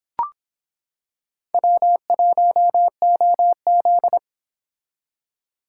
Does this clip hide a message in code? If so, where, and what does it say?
Yes: Morse code from 1.5 to 4.2 s, heard as a keyed tone, about 26 wpm, 700 Hz. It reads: W1OZ